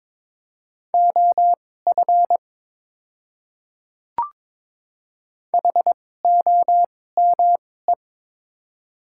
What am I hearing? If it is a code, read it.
Morse code, heard as a keyed tone, about 22 wpm, spaced as for 16 wpm, 700 Hz. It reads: OF HOME
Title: 2x Character Spacing - Top 300 Words - 22wpm